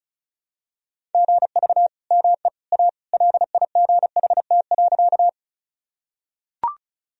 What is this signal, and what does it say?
Morse code, heard as a keyed tone, about 36 wpm, 700 Hz. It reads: GV ME A LIGHT.
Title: Sentences from Top 200 Words and Abbreviations 35wpm